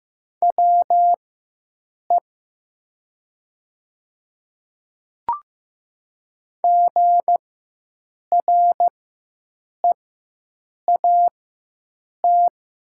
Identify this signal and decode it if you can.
Morse code, heard as a keyed tone, about 15 wpm, spaced as for 7 wpm, 700 Hz. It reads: WE GREAT
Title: 4x Character Spacing - Top 200 Words - 15wpm